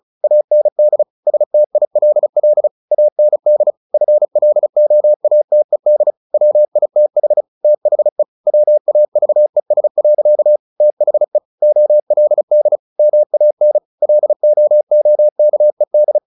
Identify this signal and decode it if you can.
Morse code, heard as a keyed tone, about 34 wpm, 600 Hz. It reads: AND STILL AND FLOATED WITH THE WAVES. THE OLD MAN LOOKED
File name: the old man and the sea 2 35wpm